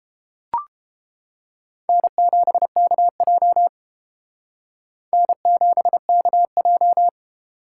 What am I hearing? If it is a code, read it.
Morse code, heard as a keyed tone, about 29 wpm, 700 Hz. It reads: N7KJ N7KJ